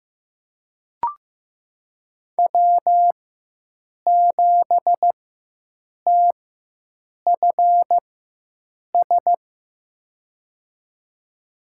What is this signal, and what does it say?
Morse code, heard as a keyed tone, about 16 wpm, spaced as for 7 wpm, 700 Hz. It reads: W7TFS
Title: USA Call Signs - 5 Characters - 4x Character Spacing - 15wpm